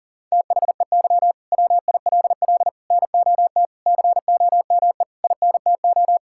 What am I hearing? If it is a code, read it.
Morse code, heard as a keyed tone, about 40 wpm, 700 Hz. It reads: THEY WILL NOT COME INTO
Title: Sentences from Top 1000 Words - Review Part 5 - 40wpm